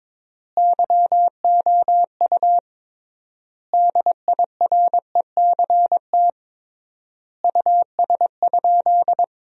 Morse code, heard as a keyed tone, about 22 wpm, 700 Hz. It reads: YOU DIRECT US?
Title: 3x Word-Spacing - Sentences from Top 400 Words - 22wpm